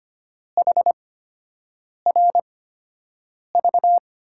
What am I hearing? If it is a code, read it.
Morse code, heard as a keyed tone, about 25 wpm, spaced as for 7 wpm, 700 Hz. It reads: HRV
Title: USA Call Signs - 8x Character Spacing - 25wpm